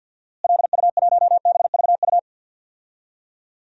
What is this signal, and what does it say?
Morse code, heard as a keyed tone, about 47 wpm, 700 Hz. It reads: LU1BVU
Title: DXCC-100 Call Signs - 6 Characters - Repeated 3x - 50wpm